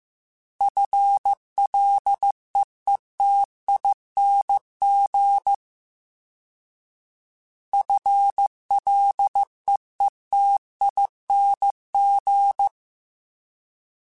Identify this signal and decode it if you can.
Morse code, heard as a keyed tone, about 15 wpm, 800 Hz. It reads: FLEETING FLEETING
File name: FL / FL mp3s / FL Prefixes 3x2 Twice @13